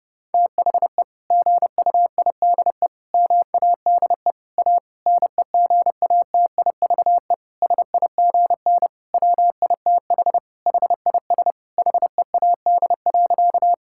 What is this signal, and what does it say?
Morse code, heard as a keyed tone, about 30 wpm, 700 Hz. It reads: THE GUIDE MADE A NEGATIVE SIGN WITH HIS HEAD.
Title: Sentences from Top 1000 Words - Review Part 6 - 30wpm